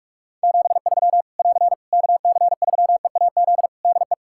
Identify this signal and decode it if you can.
Morse code, heard as a keyed tone, about 41 wpm, 700 Hz. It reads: 73 <AR> KC3EAZ DE
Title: Practice QSO - Exchange 4 - Ending - 45wpm